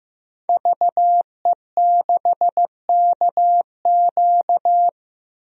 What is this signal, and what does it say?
Morse code, heard as a keyed tone, about 15 wpm, 700 Hz. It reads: VE6KQ
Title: DXCC-100 Call Signs - 5 Characters - 15wpm